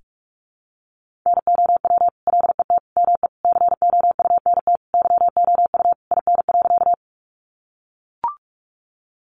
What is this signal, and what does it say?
Morse code, heard as a keyed tone, about 44 wpm, 700 Hz. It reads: NOW LET ME COUNT YOU IN.
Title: Sentences from Top 1500 Words 45wpm